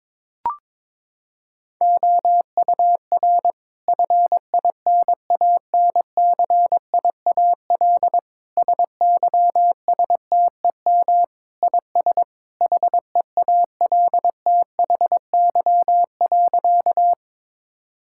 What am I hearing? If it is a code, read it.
Morse code, heard as a keyed tone, about 22 wpm, 700 Hz. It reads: OUR FINANCIAL SYSTEM IS HEALTHY.
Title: Sentences from Top 1000 Words - Review Part 5 - 22wpm